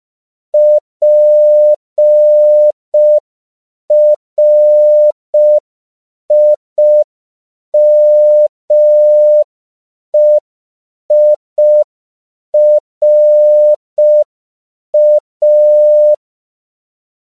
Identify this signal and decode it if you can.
Morse code, heard as a keyed tone, about 5 wpm, 600 Hz. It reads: PRIMEIRA
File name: CW / 5PPM